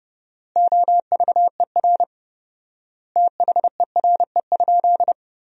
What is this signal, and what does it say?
Morse code, heard as a keyed tone, about 30 wpm, 700 Hz. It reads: OVER THERE?
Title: 4x Word-Spacing - Sentences from Top 200 Words - 30wpm